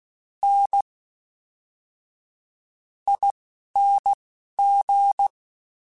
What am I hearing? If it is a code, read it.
Morse code, heard as a keyed tone, about 16 wpm, spaced as for 12 wpm, 800 Hz. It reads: N ING
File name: Special INGs / Special INGs-mp3s / Fast ing @10